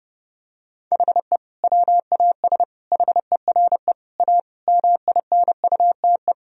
Morse code, heard as a keyed tone, about 32 wpm, 700 Hz. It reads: HE WAS HERE A MINUTE
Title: Sentences from Top 1000 Words - Review Part 3 - 30wpm